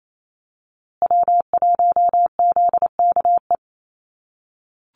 Morse code, heard as a keyed tone, about 28 wpm, 700 Hz. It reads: W1ZKE
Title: USA Call Signs Repeated 3x 28wpm